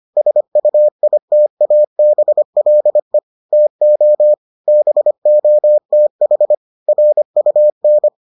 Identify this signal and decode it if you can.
Morse code, heard as a keyed tone, about 25 wpm, 600 Hz. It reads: SUITABLE TO BOTH RUN